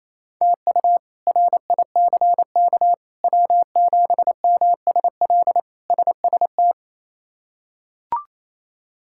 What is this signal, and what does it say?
Morse code, heard as a keyed tone, about 28 wpm, 700 Hz. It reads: TU RICK W7MSL SST
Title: K1USN SST QSO Practice - 28wpm